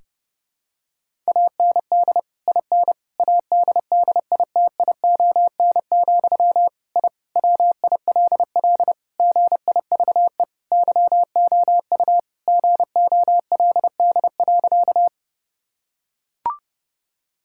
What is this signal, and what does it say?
Morse code, heard as a keyed tone, about 30 wpm, 700 Hz. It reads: AND IN ADDITION, I WILL GIVE YOU GOLD.
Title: Sentences from Top 1000 Words - Review Part 6 - 30wpm